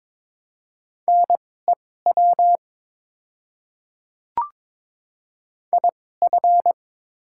Morse code, heard as a keyed tone, about 22 wpm, spaced as for 16 wpm, 690 Hz. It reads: NEW IF
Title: 2x Character Spacing - Top 500 Words - 22wpm